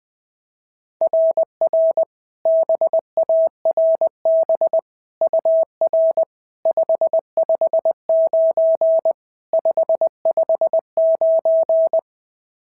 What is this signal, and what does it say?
Morse code, heard as a keyed tone, about 20 wpm, 690 Hz. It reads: RR BARB UR 559 559
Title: SOTA QSO Practice 20wpm